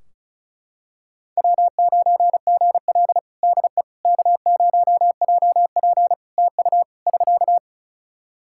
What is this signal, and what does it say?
Morse code, heard as a keyed tone, about 34 wpm, 700 Hz. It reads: W9GL DE K0JP TU <SK>
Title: Practice QSO - Exchange 4 - Ending - 35wpm